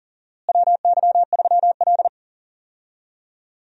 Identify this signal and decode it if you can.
Morse code, heard as a keyed tone, about 40 wpm, 700 Hz. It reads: WY3L
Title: USA Call Signs - 4 Characters - Repeated 3x - 40wpm